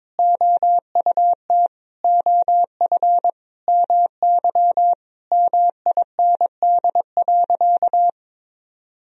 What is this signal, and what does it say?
Morse code, heard as a keyed tone, about 22 wpm, 700 Hz. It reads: OUT OF MY MIND.
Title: Sentences from Top 1000 Words - Review Part 4 - 22wpm